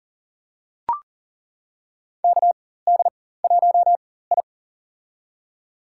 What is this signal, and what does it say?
Morse code, heard as a keyed tone, about 40 wpm, spaced as for 19 wpm, 700 Hz. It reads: KD1I
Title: USA Call Signs - 4x Character Spacing - 40wpm